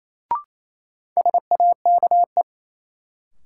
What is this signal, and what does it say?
Morse code, heard as a keyed tone, about 27 wpm, 700 Hz. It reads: SAKE
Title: Sets of 4 Words - New Word Drill - 28wpm